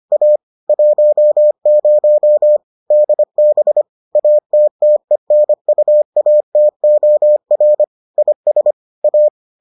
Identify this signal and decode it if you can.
Morse code, heard as a keyed tone, about 25 wpm, 600 Hz. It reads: A 10 DB ATTENUATOR IS A